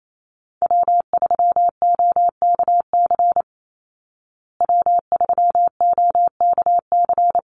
Morse code, heard as a keyed tone, about 28 wpm, 700 Hz. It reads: W3OKC W3OKC
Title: Call Signs Repeated 2x - 28wpm